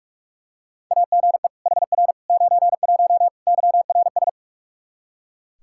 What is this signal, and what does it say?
Morse code, heard as a keyed tone, about 42 wpm, 700 Hz. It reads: AGE HR 91 YRS
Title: Practice QSO - Exchange 3 - More Info - 45wpm